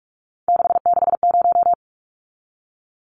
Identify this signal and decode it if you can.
Morse code, heard as a keyed tone, about 44 wpm, 700 Hz. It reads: 660
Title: Numbers - 3-Digit 45wpm